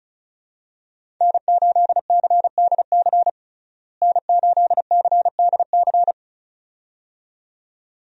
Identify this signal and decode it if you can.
Morse code, heard as a keyed tone, about 36 wpm, 700 Hz. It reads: N8CDC N8CDC
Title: USA Call Signs - 5 Characters - Repeated 2x - 35wpm